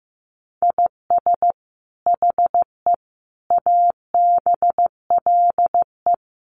Morse code, heard as a keyed tone, about 15 wpm, 700 Hz. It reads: IS HE ABLE
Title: Sentences from Top 1400 Words 15wpm